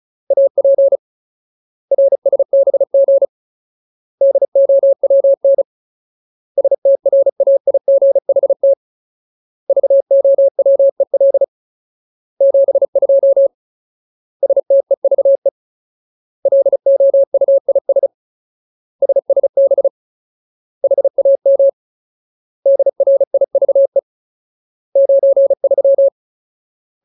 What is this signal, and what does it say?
Morse code, heard as a keyed tone, about 35 wpm, 550 Hz. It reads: AP RSBG DOWN STRAIGHT VOWEL 72 STEVE LOUIS SSB HAM DRIVE 93
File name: Ultimate Word List / 35WPM Ultimate